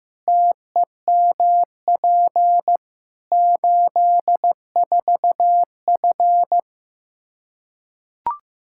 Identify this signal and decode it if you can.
Morse code, heard as a keyed tone, about 15 wpm, 700 Hz. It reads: TEMP 84F
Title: Practice QSO - Exchange 2 - Info - 15wpm